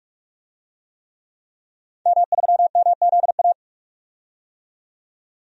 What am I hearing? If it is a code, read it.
Morse code, heard as a keyed tone, about 46 wpm, 700 Hz. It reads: M3MZA
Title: DXCC-100 Call Signs - 5 Characters - 45wpm